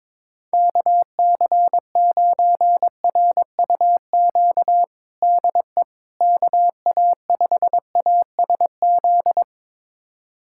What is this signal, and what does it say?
Morse code, heard as a keyed tone, about 22 wpm, 700 Hz. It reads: KC9RUQ DE KA5ASZ
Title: USA Callsign DE Callsign - 6 Characters - 22wpm